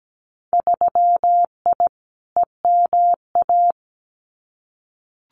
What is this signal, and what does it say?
Morse code, heard as a keyed tone, about 17 wpm, 700 Hz. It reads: 3I EMA
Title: Winter Field Day Exchanges Encore 17wpm